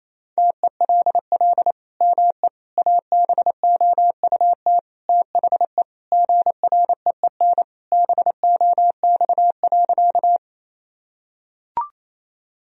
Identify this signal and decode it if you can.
Morse code, heard as a keyed tone, about 27 wpm, 700 Hz. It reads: TELL ME ABOUT THE GREEN BOX.
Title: Sentences from Top 1000 Words - Review Part 2 - 28wpm